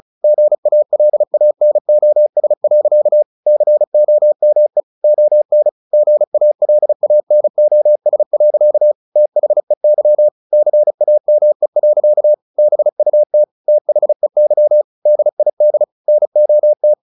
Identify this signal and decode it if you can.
Morse code, heard as a keyed tone, about 35 wpm, 600 Hz. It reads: GALANOS. COME ON GALANOS. THEY CAME. BUT THEY DID NOT
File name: the old man and the sea 2 35wpm